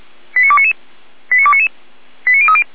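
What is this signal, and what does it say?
Five-tone selective call (ZVEI), heard as quick runs of five tones, tones of about 70 ms. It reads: 89299 89299 89030
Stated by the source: ZVEI five-tone selective call, received on an SDR.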